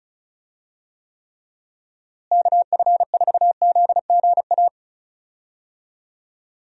Morse code, heard as a keyed tone, about 36 wpm, 700 Hz. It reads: KF4ZGA
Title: USA Call Signs - 6 Characters - Repeated 3x - 35wpm